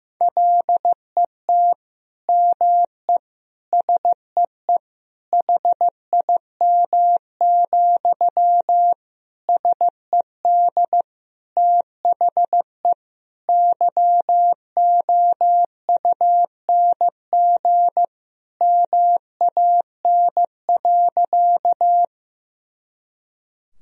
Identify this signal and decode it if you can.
Morse code, heard as a keyed tone, about 15 wpm, 700 Hz. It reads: LET ME SEE HIM, SED THE YOUNG MAN.
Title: Sentences from Top 400 Words and Abbreviations 15wpm